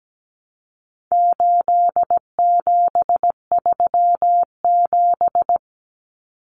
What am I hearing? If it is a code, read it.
Morse code, heard as a keyed tone, about 17 wpm, 700 Hz. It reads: 8737
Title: Common Cut Numbers - 4-Digit 17wpm